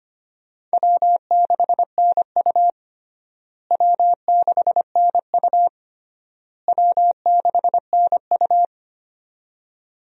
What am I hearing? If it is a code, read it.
Morse code, heard as a keyed tone, about 25 wpm, 700 Hz. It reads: W6NU W6NU W6NU
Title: USA Call Signs - 4 Characters - Repeated 3x - 25wpm